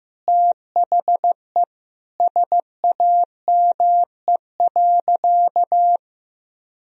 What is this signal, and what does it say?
Morse code, heard as a keyed tone, about 15 wpm, 700 Hz. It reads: THE SAME.
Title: Sentences from Top 1000 Words - Review Part 3 - 15wpm